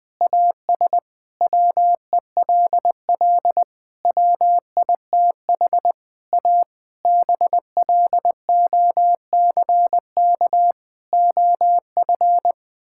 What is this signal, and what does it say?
Morse code, heard as a keyed tone, about 19 wpm, 700 Hz. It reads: AS WELL WITH A BLOCK OF
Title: Sentences from Top 1600 Words 20wpm